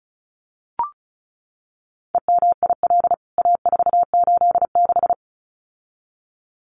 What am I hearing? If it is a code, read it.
Morse code, heard as a keyed tone, about 34 wpm, 700 Hz. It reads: EMIL A486
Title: CWT Exchange 35wpm